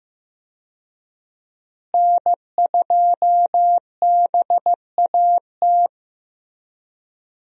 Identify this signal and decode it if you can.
Morse code, heard as a keyed tone, about 15 wpm, 700 Hz. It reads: N2BAT
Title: USA Call Signs - 5 Characters - Repeated 2x - 15wpm